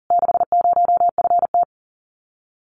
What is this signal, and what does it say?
Morse code, heard as a keyed tone, about 37 wpm, 700 Hz. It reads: 60FT
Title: Common QSO Phrases Encore 40wpm